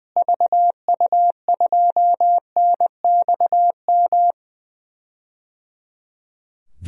Morse code, heard as a keyed tone, about 20 wpm, 700 Hz. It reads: VU2NXM
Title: DXCC-100 Call Signs - 6 Characters - Repeated 3x - 20wpm